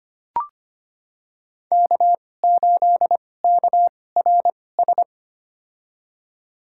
Morse code, heard as a keyed tone, about 25 wpm, spaced as for 18 wpm, 700 Hz. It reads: K8KRS
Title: USA Call Signs - 5 Characters - 2x Character Spacing - 25wpm